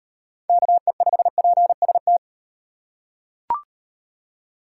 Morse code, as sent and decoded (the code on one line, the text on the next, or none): -.- . ..... .--. ... -
KE5PST